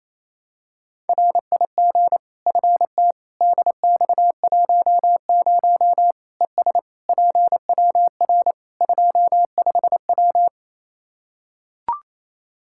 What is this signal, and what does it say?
Morse code, heard as a keyed tone, about 27 wpm, 700 Hz. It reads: RIG FT DX10 ES PWR 25W
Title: Practice QSO - Exchange 2 - Info - 28wpm